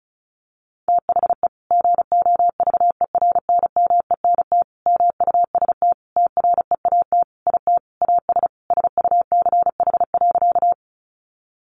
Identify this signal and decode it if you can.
Morse code, heard as a keyed tone, about 34 wpm, 700 Hz. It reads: THE GOVERNMENT MUST TREAT IT AS SUCH.
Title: Sentences from Top 1300 Words 35wpm